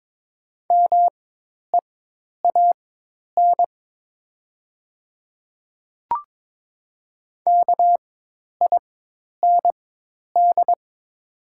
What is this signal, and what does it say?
Morse code, heard as a keyed tone, about 22 wpm, spaced as for 10 wpm, 700 Hz. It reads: MEAN KIND
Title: 4x Character Spacing - Top 300 Words - 22wpm